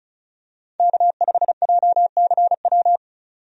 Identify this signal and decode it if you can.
Morse code, heard as a keyed tone, about 36 wpm, 700 Hz. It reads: K5JCW
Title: USA Call Signs - 5 Characters - Repeated 3x - 35wpm